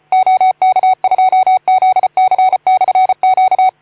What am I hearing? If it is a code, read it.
Morse code, heard as a keyed tone, about 34 wpm, 750 Hz. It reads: OK2ZC/Q
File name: Archiv / CUC181 / OK2ZC CUC 181